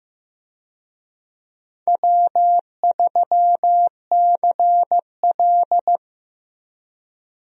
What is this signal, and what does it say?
Morse code, heard as a keyed tone, about 15 wpm, 700 Hz. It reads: W3CL
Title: USA Call Signs - 4 Characters - 15wpm